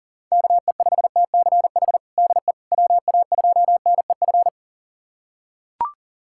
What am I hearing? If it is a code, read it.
Morse code, heard as a keyed tone, about 40 wpm, 700 Hz. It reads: KE5TCH DE WA2NEF
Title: USA Callsign DE Callsign 40wpm